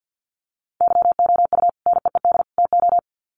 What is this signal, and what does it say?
Morse code, heard as a keyed tone, about 50 wpm, 700 Hz. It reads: YOU NEED TO
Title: Sets of 3 Words 50wpm